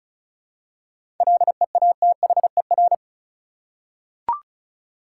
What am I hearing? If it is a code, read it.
Morse code, heard as a keyed tone, about 34 wpm, 700 Hz. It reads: LEATHER
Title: Seven Letter Words - 35wpm